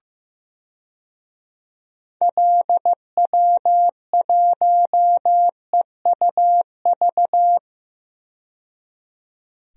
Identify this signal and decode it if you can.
Morse code, heard as a keyed tone, about 15 wpm, 700 Hz. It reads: LW1EUV
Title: DXCC-100 Call Signs - 6 Characters - 15wpm